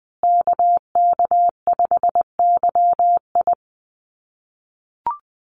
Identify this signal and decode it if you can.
Morse code, heard as a keyed tone, about 20 wpm, 700 Hz. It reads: KK5YI
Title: USA Call Signs Repeated 2x 20wpm